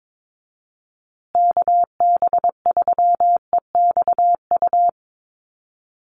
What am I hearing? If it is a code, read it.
Morse code, heard as a keyed tone, about 22 wpm, 700 Hz. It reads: KB3EXU
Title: USA Call Signs Repeated 2x 22wpm